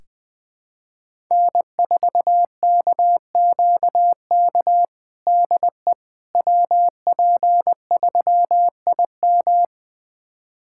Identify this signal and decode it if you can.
Morse code, heard as a keyed tone, about 20 wpm, 700 Hz. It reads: N4KQK DE WP3IM
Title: USA Callsign DE Callsign - 5 Characters - 20wpm